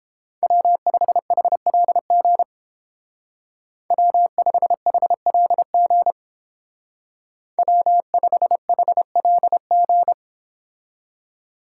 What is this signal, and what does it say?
Morse code, heard as a keyed tone, about 30 wpm, 700 Hz. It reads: W5HLG W5HLG W5HLG